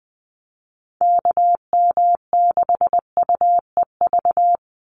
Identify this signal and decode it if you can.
Morse code, heard as a keyed tone, about 20 wpm, 700 Hz. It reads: KM6UEV
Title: USA Call Signs 20wpm